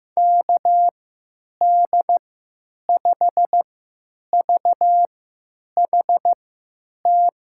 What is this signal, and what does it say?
Morse code, heard as a keyed tone, about 15 wpm, spaced as for 9 wpm, 700 Hz. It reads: KD5VHT
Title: USA Call Signs - 6 Characters - 3x Character Spacing - 15wpm